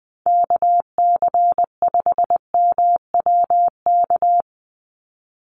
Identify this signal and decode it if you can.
Morse code, heard as a keyed tone, about 20 wpm, 700 Hz. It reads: KC5MWK
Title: USA Call Signs 20wpm